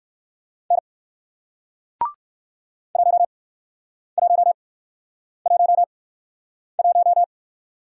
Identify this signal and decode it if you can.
Morse code, heard as a keyed tone, about 61 wpm, spaced as for 9 wpm, 700 Hz. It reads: N 1111